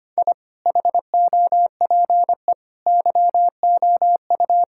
Morse code, heard as a keyed tone, about 25 wpm, 700 Hz. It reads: I HOPE YOU
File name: Sets of 9 Words - 25wpm